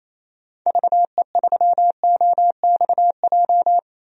Morse code, heard as a keyed tone, about 28 wpm, 700 Hz. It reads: VE3OXJ
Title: DXCC-100 Call Signs - 6 Characters - Repeated 3x - 28wpm